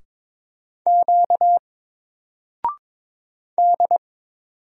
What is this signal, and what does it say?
Morse code, heard as a keyed tone, about 22 wpm, 700 Hz. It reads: Q D
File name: Single-Letter-Number-22wpm